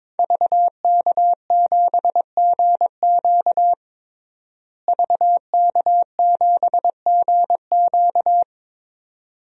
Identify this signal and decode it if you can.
Morse code, heard as a keyed tone, about 22 wpm, 700 Hz. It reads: VK7GQ VK7GQ
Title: DXCC-100 Call Signs - 5 Characters - Repeated 2x - 22wpm